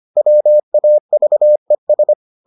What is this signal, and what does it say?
Morse code, heard as a keyed tone, about 25 wpm, 600 Hz. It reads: WAVES